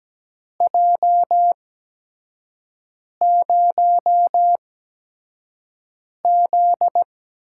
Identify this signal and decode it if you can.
Morse code, heard as a keyed tone, about 17 wpm, spaced as for 5 wpm, 700 Hz. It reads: J0Z
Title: USA Call Signs - 4 Characters - 8x Character Spacing - 17wpm